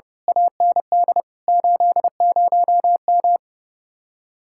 Morse code, heard as a keyed tone, about 34 wpm, 700 Hz. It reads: AND 80M
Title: Speed-Racing Common QSO Phrases - 30wpm